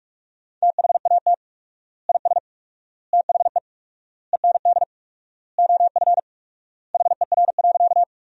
Morse code, heard as a keyed tone, about 46 wpm, 700 Hz. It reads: THAT IS THE END OF HER.
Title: 4x Word-Spacing - Sentences from Top 200 Words - 45wpm